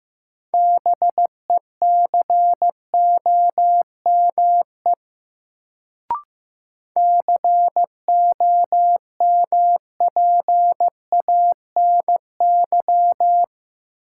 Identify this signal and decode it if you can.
Morse code, heard as a keyed tone, about 15 wpm, 700 Hz. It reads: BECOME COMPANY
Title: Sentences from Top 600 Words - New Word Drill - 15wpm